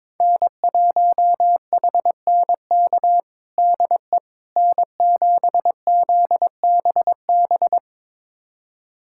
Morse code, heard as a keyed tone, about 22 wpm, 700 Hz. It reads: N1HNK DE N7ZBB
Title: USA Callsign DE Callsign - 5 Characters - 22wpm